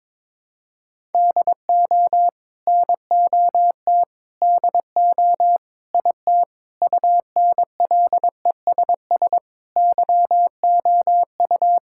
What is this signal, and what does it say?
Morse code, heard as a keyed tone, about 22 wpm, 700 Hz. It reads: DO NOT DO IT UNLESS YOU
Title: Sentences from Top 1000 Words - Review Part 6 - 22wpm